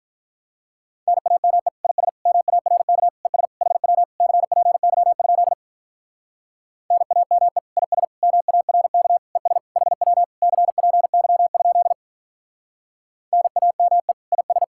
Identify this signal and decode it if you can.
Morse code, heard as a keyed tone, about 48 wpm, 700 Hz. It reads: NAME IS MARK ES HW CPY? NAME IS MARK ES HW CPY? NAME IS